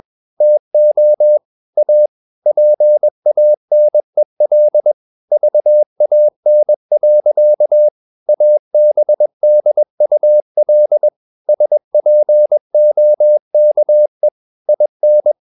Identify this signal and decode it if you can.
Morse code, heard as a keyed tone, about 21 wpm, 600 Hz. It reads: TO A PANEL VAN. ABDUL SPOKE IN
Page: CW med Gnister